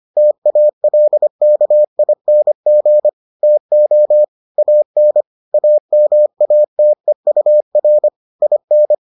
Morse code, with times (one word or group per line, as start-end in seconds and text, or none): 0.2-3.1 TALKING
3.4-4.2 TO
4.6-5.2 AN
5.5-8.1 AMATEUR
8.4-8.9 IN